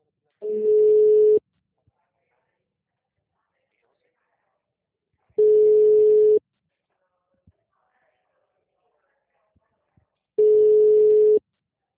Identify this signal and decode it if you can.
Ringback tone (Europe, 425 Hz), with speech.
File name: out- -2021- - - .12155